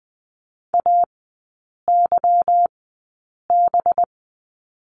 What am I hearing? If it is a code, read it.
Morse code, heard as a keyed tone, about 20 wpm, spaced as for 8 wpm, 690 Hz. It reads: AYB
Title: 191 Lesson 31e - Letter v - Three Characters 20wpm